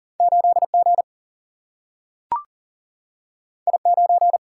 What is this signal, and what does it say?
Morse code, heard as a keyed tone, about 40 wpm, 700 Hz. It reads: 8G I9
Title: Single-Letter Single-Number 40wpm